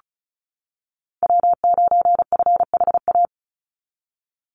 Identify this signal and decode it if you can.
Morse code, heard as a keyed tone, about 34 wpm, 700 Hz. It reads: W9FHA
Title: NA SOTA Chaser Callsigns 35wpm